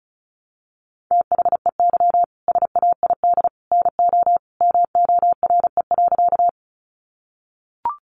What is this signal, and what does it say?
Morse code, heard as a keyed tone, about 34 wpm, 700 Hz. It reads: THEY SAID NO MORE.
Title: Sentences from Top 100 Words 35wpm